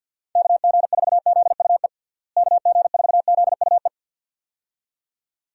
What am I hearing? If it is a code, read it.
Morse code, heard as a keyed tone, about 53 wpm, 700 Hz. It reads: KG4ZUE KG4ZUE
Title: USA Call Signs - 6 Characters - Repeated 2x - 50wpm